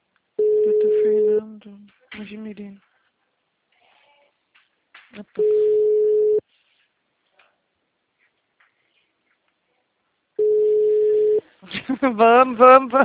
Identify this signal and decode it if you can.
Ringback tone (Europe, 425 Hz), with speech.
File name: out- -2012- - - .31013